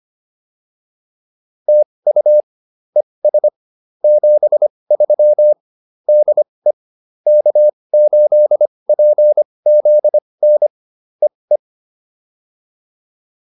Morse code, heard as a keyed tone, about 25 wpm, spaced as for 20 wpm, 600 Hz. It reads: TU ES 73 DE K8PZN EE